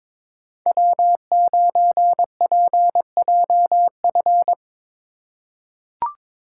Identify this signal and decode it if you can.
Morse code, heard as a keyed tone, about 22 wpm, 700 Hz. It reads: W9PJF
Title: USA Call Signs - 5 Characters - 2x Character Spacing - 22wpm